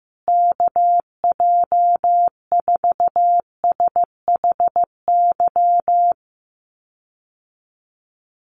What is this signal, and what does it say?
Morse code, heard as a keyed tone, about 15 wpm, 700 Hz. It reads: KJ4SHY
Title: USA Call Signs Repeated 3x 15wpm